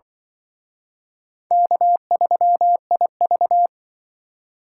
Morse code, heard as a keyed tone, about 23 wpm, 700 Hz. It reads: K3IV
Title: Speed Racing - USA 1x2 Call Signs 20wpm